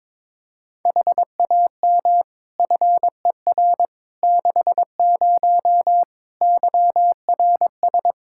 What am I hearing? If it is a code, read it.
Morse code, heard as a keyed tone, about 21 wpm, 700 Hz. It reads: HAM FER 60 YRS
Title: Practice QSO - Exchange 3 - More Info - 22wpm